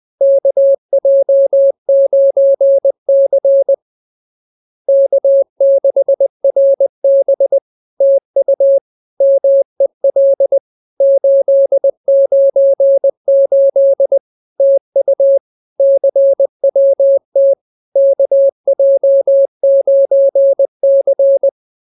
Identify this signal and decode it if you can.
Morse code, heard as a keyed tone, about 20 wpm, 550 Hz. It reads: KJ9C K6RB TU MEL 898 TU CWT KJ9C